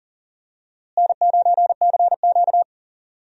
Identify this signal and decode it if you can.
Morse code, heard as a keyed tone, about 40 wpm, 700 Hz. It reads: N9CQ
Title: USA Call Signs - 4 Characters - Repeated 2x - 40wpm